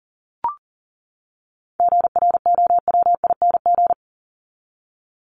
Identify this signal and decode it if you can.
Morse code, heard as a keyed tone, about 40 wpm, 700 Hz. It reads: GROWING
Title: ING Suffix - Top 100 Words with Base Words 40wpm